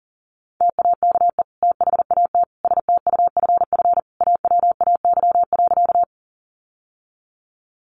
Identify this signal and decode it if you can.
Morse code, heard as a keyed tone, about 40 wpm, 700 Hz. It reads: TAKE THAT STUFF AWAY.
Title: Sentences from Top 1400 Words 40wpm